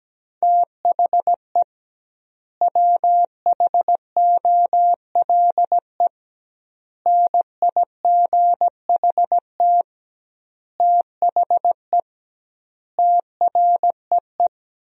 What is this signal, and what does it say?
Morse code, heard as a keyed tone, about 17 wpm, 700 Hz. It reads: THE WHOLE NIGHT THE TREE
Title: 2x Word-Spacing - Sentences from Top 500 Words - 17wpm